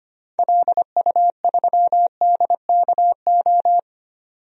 Morse code, heard as a keyed tone, about 25 wpm, 700 Hz. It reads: LU3DKO
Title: DXCC-100 Call Signs - 6 Characters - 25wpm